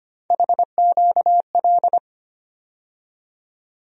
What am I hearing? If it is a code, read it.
Morse code, heard as a keyed tone, about 25 wpm, 700 Hz. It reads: HQL
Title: USA Call Sign Postfixes - 3 Letters 25wpm